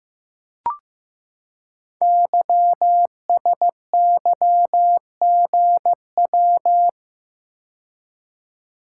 Morse code, heard as a keyed tone, about 15 wpm, 700 Hz. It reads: YSYGW
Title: Code Groups - 5 Letters - 15wpm